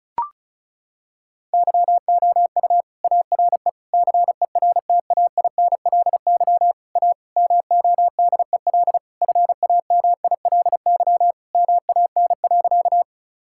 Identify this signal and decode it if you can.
Morse code, heard as a keyed tone, about 36 wpm, 700 Hz. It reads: YOU ARE CERTAINLY A MODEL FAMILY MAN.
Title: Sentences from Top 1000 Words - Review Part 5 - 35wpm